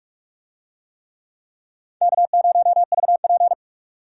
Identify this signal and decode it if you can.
Morse code, heard as a keyed tone, about 46 wpm, 700 Hz. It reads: K0VP